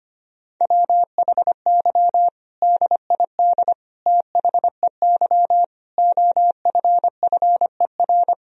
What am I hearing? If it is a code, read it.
Morse code, heard as a keyed tone, about 25 wpm, 700 Hz. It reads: WHY DID THEY OFFER
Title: Sentences from Top 1000 Words - Review Part 4 - 25wpm